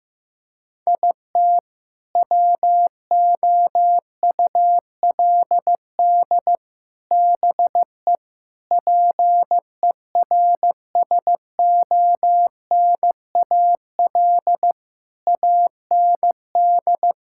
Morse code, heard as a keyed tone, about 15 wpm, 700 Hz. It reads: IT WOULD BE PERSONAL AND
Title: Sentences from Top 1000 Words - Review Part 5 - 15wpm